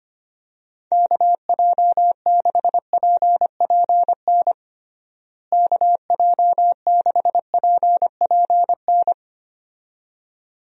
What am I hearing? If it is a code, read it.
Morse code, heard as a keyed tone, about 25 wpm, 700 Hz. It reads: KJ6PPN KJ6PPN
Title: USA Call Signs - 6 Characters - Repeated 2x - 25wpm